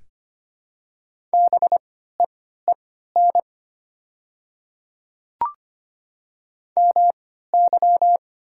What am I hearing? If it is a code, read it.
Morse code, heard as a keyed tone, about 25 wpm, spaced as for 14 wpm, 700 Hz. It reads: BEEN MY
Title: 3x Character Spacing - Top 300 Words - 25wpm